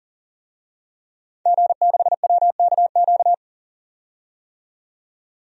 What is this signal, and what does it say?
Morse code, heard as a keyed tone, about 40 wpm, 700 Hz. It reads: G6WKQ
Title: DXCC-100 Call Signs - 5 Characters - 40wpm